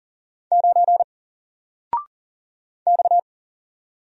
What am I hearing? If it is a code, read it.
Morse code, heard as a keyed tone, about 40 wpm, 700 Hz. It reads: 9 X
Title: Single Character (Letter:Number:Punctuation:Prosign) Alt 40wpm